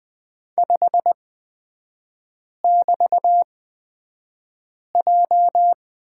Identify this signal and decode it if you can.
Morse code, heard as a keyed tone, about 20 wpm, spaced as for 5 wpm, 700 Hz. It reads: 5<BT>J
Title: 224 Lesson 36e - BT - Three Characters 20wpm